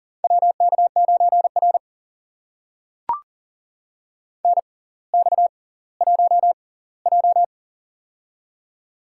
Morse code, heard as a keyed tone, about 40 wpm, spaced as for 20 wpm, 700 Hz. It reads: WK9R NX1J
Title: USA Call Signs - 4 Characters - 6x Character Spacing - 40wpm